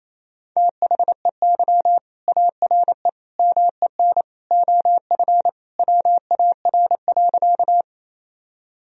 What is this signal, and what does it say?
Morse code, heard as a keyed tone, about 28 wpm, 700 Hz. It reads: THEY ARE MEN OF WAR.
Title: Sentences from Top 1000 Words - Review Part 2 - 28wpm